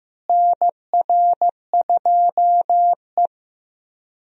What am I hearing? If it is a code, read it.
Morse code, heard as a keyed tone, about 16 wpm, 700 Hz. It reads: NR2E
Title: USA Call Signs - 4 Characters - Repeated 3x - 15wpm